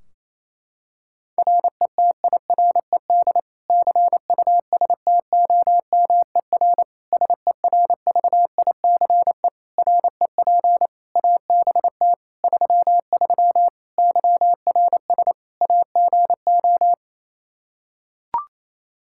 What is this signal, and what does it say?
Morse code, heard as a keyed tone, about 27 wpm, 700 Hz. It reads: RETIRED CUSTOMER SERVICE REP ABT 33 YRS AGO
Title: Practice QSO - Exchange 3 - More Info - 28wpm